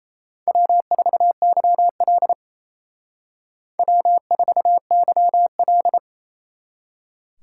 Morse code, heard as a keyed tone, about 35 wpm, 700 Hz. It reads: W4YL W4YL